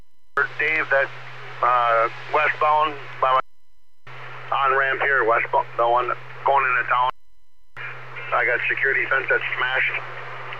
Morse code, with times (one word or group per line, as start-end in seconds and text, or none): none